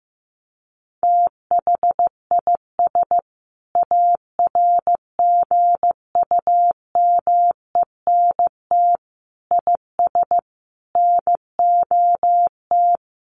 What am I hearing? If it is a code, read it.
Morse code, heard as a keyed tone, about 15 wpm, 700 Hz. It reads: THIS ARGUMENT IS NOT
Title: Sentences from Top 1400 Words 15wpm